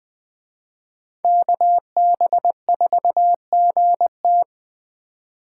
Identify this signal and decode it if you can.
Morse code, heard as a keyed tone, about 21 wpm, 700 Hz. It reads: KB4GT